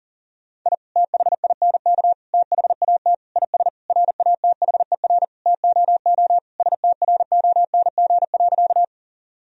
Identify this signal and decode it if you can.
Morse code, heard as a keyed tone, about 40 wpm, 700 Hz. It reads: I THINK THAT IS RATHER TOO STRONG.
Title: Sentences from Top 1000 Words - Review Part 4 - 40wpm